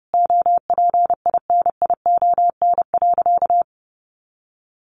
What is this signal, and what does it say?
Morse code, heard as a keyed tone, about 30 wpm, 700 Hz. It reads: OPINION.
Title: A Sound Of Thunder 30wpm